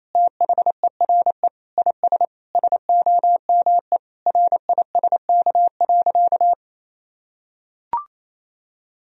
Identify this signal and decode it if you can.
Morse code, heard as a keyed tone, about 27 wpm, 700 Hz. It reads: THERE IS SOME RISK.
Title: Sentences from Top 1000 Words - Review Part 4 - 28wpm